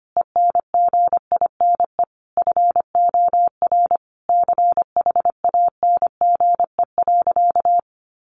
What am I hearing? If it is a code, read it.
Morse code, heard as a keyed tone, about 25 wpm, 700 Hz. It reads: ENGINE FOR CHANGE.
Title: Sentences from Top 1500 Words 25wpm